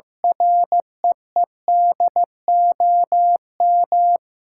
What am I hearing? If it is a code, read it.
Morse code, heard as a keyed tone, about 16 wpm, 700 Hz. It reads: REEDOM
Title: Sentences from Top 1600 Words 15wpm